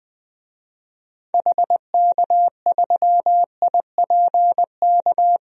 Morse code, heard as a keyed tone, about 21 wpm, 700 Hz. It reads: HK3IPK
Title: DXCC-100 Call Signs - 6 Characters - Repeated 3x - 20wpm